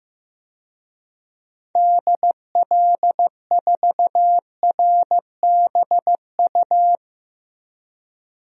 Morse code, heard as a keyed tone, about 15 wpm, 700 Hz. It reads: DL4RBU
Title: DXCC-100 Call Signs - 6 Characters - Repeated 2x - 15wpm